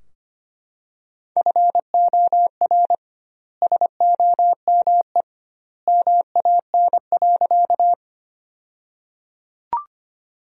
Morse code, heard as a keyed tone, about 25 wpm, 700 Hz. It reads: FOR SOME MAN.
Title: 2x Word-Spacing - Sentences from Top 200 Words - 25wpm